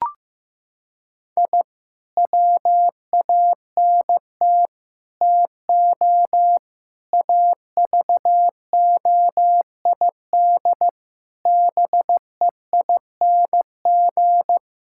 Morse code, heard as a keyed tone, about 15 wpm, 700 Hz. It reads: I WANT TO AVOID BEING
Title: Sentences from Top 1000 Words - Review Part 6 - 15wpm